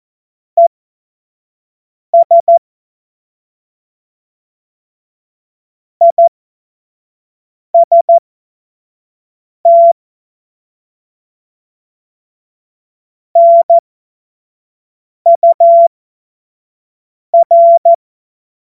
Morse code, heard as a keyed tone, about 14 wpm, spaced as for 5 wpm, 680 Hz. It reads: ES IST NUR
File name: Morsekurs / Witze / Witz 17. August 2021, 25 F70